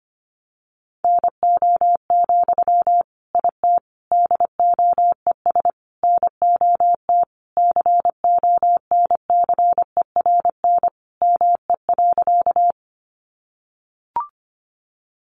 Morse code, heard as a keyed tone, about 25 wpm, 700 Hz. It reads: NO, IT DOES NOT CONCERN ME.
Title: Sentences from Top 1000 Words 25wpm